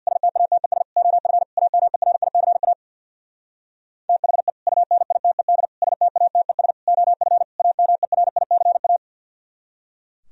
Morse code, heard as a keyed tone, about 61 wpm, 700 Hz. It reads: STATES OF AMERICA THE UNITED STATES OF AMERICA
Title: Speed-Racing - Sets of 5 Words 50wpm